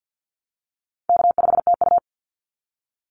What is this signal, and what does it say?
Morse code, heard as a keyed tone, about 50 wpm, 700 Hz. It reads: K5TU
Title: Call Signs Repeated 2x - 50wpm